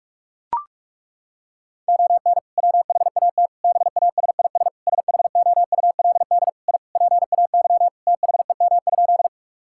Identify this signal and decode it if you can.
Morse code, heard as a keyed tone, about 42 wpm, 700 Hz. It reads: ON WHAT BASIS SHOULD I PAY THEM?
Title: Sentences from Top 1000 Words - Review Part 5 - 45wpm